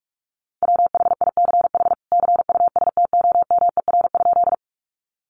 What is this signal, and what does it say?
Morse code, heard as a keyed tone, about 44 wpm, 700 Hz. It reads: WHICH CUSTOMER?
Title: Sentences from Top 1100 Words 45wpm